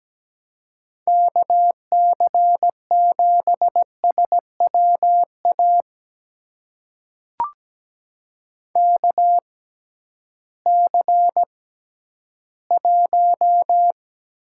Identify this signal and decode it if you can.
Morse code, heard as a keyed tone, about 17 wpm, spaced as for 11 wpm, 700 Hz. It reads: KC7SWA KC1
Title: USA Call Signs - 6 Characters - 6x Character Spacing - 17wpm